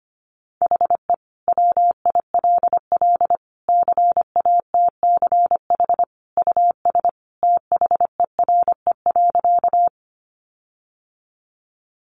Morse code, heard as a keyed tone, about 25 wpm, 700 Hz. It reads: HE WILL CATCH US THERE.
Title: Sentences from Top 1500 Words 25wpm